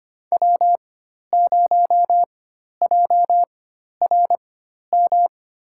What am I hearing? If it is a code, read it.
Morse code, heard as a keyed tone, about 26 wpm, spaced as for 12 wpm, 690 Hz. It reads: W0JRM
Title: USA Call Signs - 4x Character Spacing - 25wpm